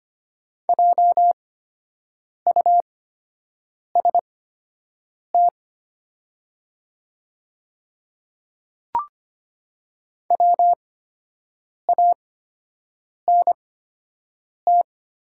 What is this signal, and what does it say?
Morse code, heard as a keyed tone, about 25 wpm, spaced as for 7 wpm, 700 Hz. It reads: JUST WANT
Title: 8x Character Spacing - Top 200 Words - 25wpm